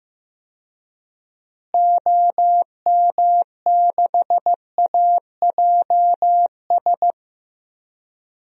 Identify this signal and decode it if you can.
Morse code, heard as a keyed tone, about 15 wpm, 700 Hz. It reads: OM6AJS
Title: DXCC-100 Call Signs - 6 Characters - Repeated 2x - 15wpm